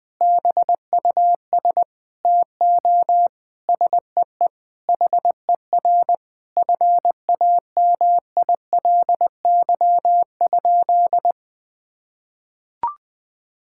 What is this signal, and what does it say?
Morse code, heard as a keyed tone, about 20 wpm, 700 Hz. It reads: BUS TO SEE HER FAMILY?
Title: Sentences from Top 1000 Words - Review Part 6 - 20wpm